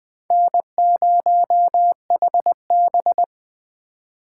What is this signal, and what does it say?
Morse code, heard as a keyed tone, about 20 wpm, 700 Hz. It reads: N0HB
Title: Speed Racing - USA 1x2 Call Signs 17wpm